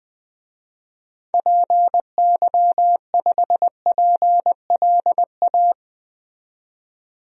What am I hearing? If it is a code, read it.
Morse code, heard as a keyed tone, about 21 wpm, 700 Hz. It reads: PY5PLA
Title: DXCC-100 Call Signs - 6 Characters - 20wpm